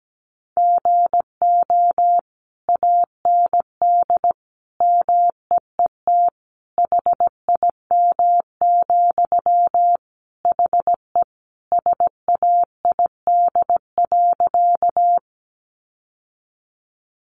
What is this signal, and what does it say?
Morse code, heard as a keyed tone, about 17 wpm, 700 Hz. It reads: GO AND MEET HIM, HE SAID.
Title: Sentences from Top 1100 Words 17wpm